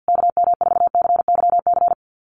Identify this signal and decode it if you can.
Morse code, heard as a keyed tone, about 50 wpm, 700 Hz. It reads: KM4CYC
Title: USA Call Signs Repeated 3x 50wpm